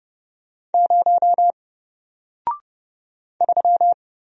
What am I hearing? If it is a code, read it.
Morse code, heard as a keyed tone, about 30 wpm, 700 Hz. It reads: 0 3
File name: Single-Letter-Number-30wpm